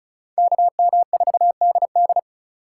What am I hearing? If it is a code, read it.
Morse code, heard as a keyed tone, about 38 wpm, 700 Hz. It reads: KM4DD